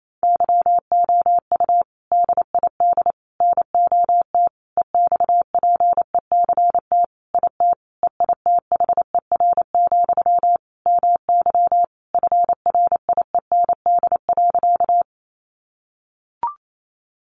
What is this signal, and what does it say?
Morse code, heard as a keyed tone, about 28 wpm, 700 Hz. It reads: YOU DID NOT EXPECT IT EITHER, MY FRIEND.
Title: Sentences from Top 1100 Words 28wpm